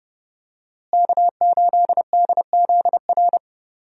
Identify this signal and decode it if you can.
Morse code, heard as a keyed tone, about 32 wpm, 700 Hz. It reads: K8DZR
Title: USA Call Signs - 5 Characters - Repeated 2x - 30wpm